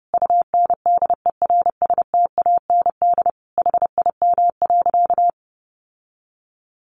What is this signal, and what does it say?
Morse code, heard as a keyed tone, about 30 wpm, 700 Hz. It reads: UNDERSTAND HIM.
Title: Sentences from Top 600 Words 30wpm